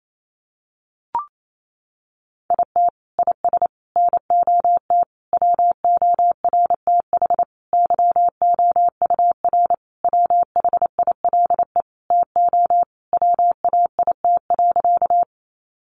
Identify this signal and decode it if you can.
Morse code, heard as a keyed tone, about 28 wpm, 700 Hz. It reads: IT IS NOT WORTH YOUR WHILE TO WAIT.
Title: Sentences from Top 800 Words 28wpm